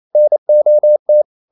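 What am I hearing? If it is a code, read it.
Morse code, heard as a keyed tone, about 28 wpm, 600 Hz. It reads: NOT
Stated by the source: Morse code practice at 28 wpm, text of Ken Follett's novel Never in English, from CW med Gnister.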